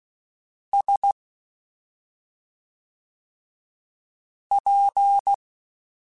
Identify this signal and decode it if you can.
Morse code, heard as a keyed tone, about 16 wpm, 800 Hz. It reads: S P
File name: Special INGs / Special INGs-mp3s / SlowToFaster ing @5